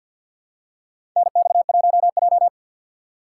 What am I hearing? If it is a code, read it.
Morse code, heard as a keyed tone, about 53 wpm, 690 Hz. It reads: NX1J
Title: USA Call Signs - 4 Characters - 6x Character Spacing - 50wpm